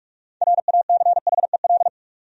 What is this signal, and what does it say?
Morse code, heard as a keyed tone, about 47 wpm, 700 Hz. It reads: RACHEL
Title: USA Common Names - 45wpm